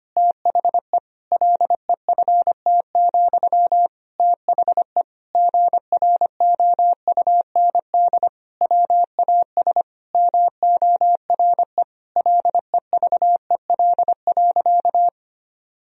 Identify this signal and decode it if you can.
Morse code, heard as a keyed tone, about 25 wpm, 700 Hz. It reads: THE LEFT, THE GROUND WAS MORE LEVEL.
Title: Sentences from Top 1000 Words - Review Part 4 - 25wpm